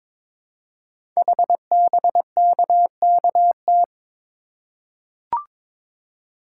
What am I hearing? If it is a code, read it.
Morse code, heard as a keyed tone, about 22 wpm, 700 Hz. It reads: HBKKT
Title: Code Groups - 5 Letters - 22wpm